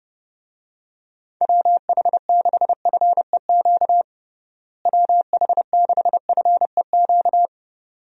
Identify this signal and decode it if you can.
Morse code, heard as a keyed tone, about 32 wpm, 700 Hz. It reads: WH6FEQ WH6FEQ
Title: USA Call Signs - 6 Characters - Repeated 2x - 30wpm